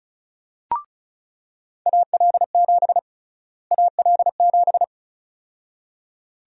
Morse code, heard as a keyed tone, about 36 wpm, 700 Hz. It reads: AL7 AL7
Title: USA Call Sign Prefixes - Repeated 2x - 35wpm